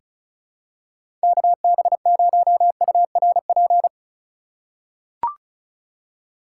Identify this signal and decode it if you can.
Morse code, heard as a keyed tone, about 36 wpm, 700 Hz. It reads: KB0URP
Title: USA Call Signs - 2x Character Spacing - 35wpm